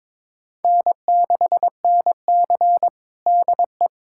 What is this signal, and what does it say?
Morse code, heard as a keyed tone, about 22 wpm, 700 Hz. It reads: N6NC DE
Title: USA Callsign DE Callsign - 4 Characters - 22wpm